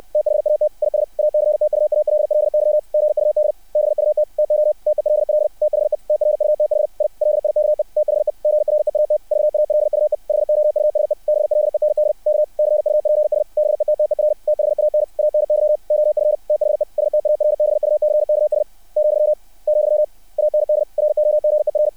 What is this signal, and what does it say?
Morse code, heard as a keyed tone, about 60 wpm, 600 Hz. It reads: THAT IS ACTUALLY GUD BUT TO TELL THE TRUTH I LIKE THE GREAT VALUE HOUSE BLEND K CUPS BETTER THAN ANY OF THE STARBUCKS , , AND ROGER
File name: archived / CRSnet